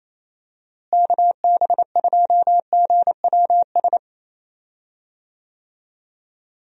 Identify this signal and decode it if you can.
Morse code, heard as a keyed tone, about 28 wpm, 700 Hz. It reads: KB2GWS
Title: USA Call Signs - 6 Characters - Repeated 2x - 28wpm